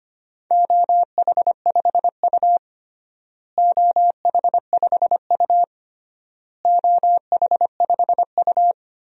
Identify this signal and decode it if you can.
Morse code, heard as a keyed tone, about 25 wpm, 700 Hz. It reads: OH5U OH5U OH5U
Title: DXCC-100 Call Signs - 4 Characters - Repeated 3x - 25wpm